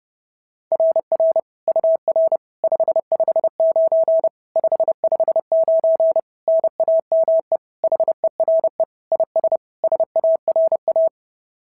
Morse code, heard as a keyed tone, about 30 wpm, 650 Hz. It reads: RR UR 559 559 NAME HERE IS SARA
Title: SOTA QSO Practice 30wpm